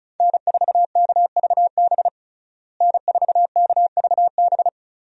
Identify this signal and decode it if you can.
Morse code, heard as a keyed tone, about 35 wpm, 700 Hz. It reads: N4KVB N4KVB
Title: USA Call Signs - 5 Characters - Repeated 3x - 35wpm